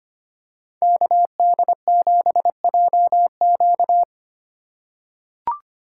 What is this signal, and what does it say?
Morse code, heard as a keyed tone, about 25 wpm, 700 Hz. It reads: KD7JQ
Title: USA Call Signs - 5 Characters - 3x Character Spacing - 25wpm